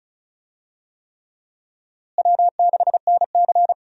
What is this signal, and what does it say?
Morse code, heard as a keyed tone, about 34 wpm, 700 Hz. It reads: W6NC
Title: USA Call Signs - 4 Characters - Repeated 3x - 35wpm